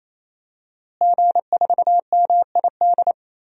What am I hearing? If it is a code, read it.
Morse code, heard as a keyed tone, about 28 wpm, 700 Hz. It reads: G4MID
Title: DXCC-100 Call Signs - 5 Characters - Repeated 3x - 28wpm